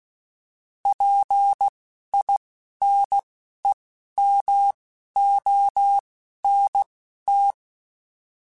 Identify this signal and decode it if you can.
Morse code, heard as a keyed tone, about 16 wpm, spaced as for 11 wpm, 800 Hz. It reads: PINEMONT